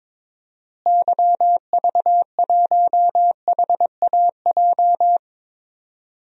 Morse code, heard as a keyed tone, about 22 wpm, 700 Hz. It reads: YV1HAJ
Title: DXCC-100 Call Signs - 6 Characters - Repeated 2x - 22wpm